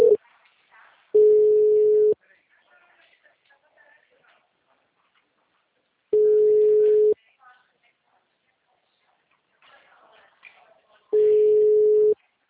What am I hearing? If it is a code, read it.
Ringback tone (Europe, 425 Hz), with speech.